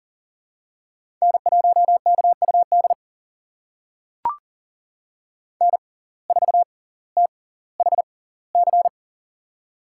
Morse code, heard as a keyed tone, about 40 wpm, spaced as for 20 wpm, 700 Hz. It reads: N1KUD N4THC
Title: USA Call Signs - 5 Characters - 6x Character Spacing - 40wpm